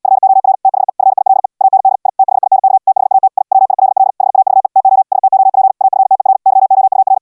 Morse code, heard as a keyed tone, about 80 wpm, 770 Hz. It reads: FOR THE FILE AND I THINK THAT I WILL HAVE TO STOP AFTER YOUR